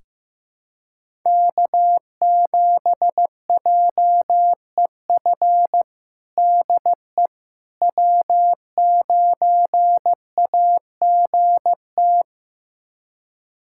Morse code, heard as a keyed tone, about 15 wpm, 700 Hz. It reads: K7JEF DE W9AGT
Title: USA Callsign DE Callsign - 5 Characters - 15wpm